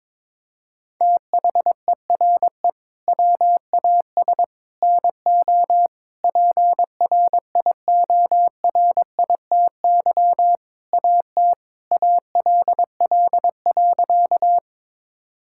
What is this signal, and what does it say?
Morse code, heard as a keyed tone, about 22 wpm, 700 Hz. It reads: THERE WAS NO PRIORITY AT ALL.
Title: Sentences from Top 1600 Words 22wpm